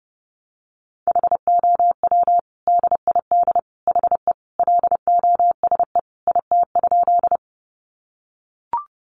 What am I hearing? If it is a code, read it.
Morse code, heard as a keyed tone, about 30 wpm, 700 Hz. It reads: HOW DID HE LOSE IT?
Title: Sentences from Top 1100 Words 30wpm